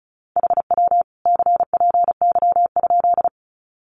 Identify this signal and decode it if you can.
Morse code, heard as a keyed tone, about 34 wpm, 700 Hz. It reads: HW CPY?
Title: Common QSO Phrases 35wpm